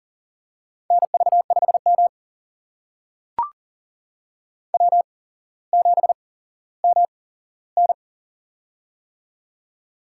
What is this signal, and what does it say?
Morse code, heard as a keyed tone, about 38 wpm, spaced as for 17 wpm, 700 Hz. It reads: NV5M W7MN
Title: USA Call Signs - 4 Characters - 8x Character Spacing - 40wpm